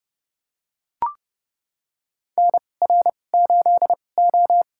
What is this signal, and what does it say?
Morse code, heard as a keyed tone, about 32 wpm, spaced as for 22 wpm, 700 Hz. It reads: NR8O
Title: USA Call Signs - 4 Characters - 2x Character Spacing - 30wpm